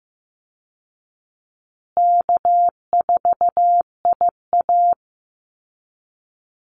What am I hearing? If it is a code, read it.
Morse code, heard as a keyed tone, about 15 wpm, 700 Hz. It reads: K4IA
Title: Easy Call Signs Repeated 3x - 15wpm